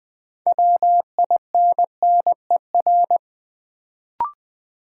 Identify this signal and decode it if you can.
Morse code, heard as a keyed tone, about 20 wpm, 700 Hz. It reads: WINNER
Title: Build A Word - 3 Letters At A Time - 6-Letter Words - 20wpm